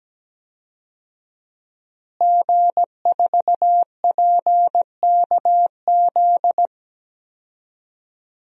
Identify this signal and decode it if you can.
Morse code, heard as a keyed tone, about 17 wpm, 700 Hz. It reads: G4PKZ
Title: DXCC-100 Call Signs - 5 Characters - Repeated 3x - 17wpm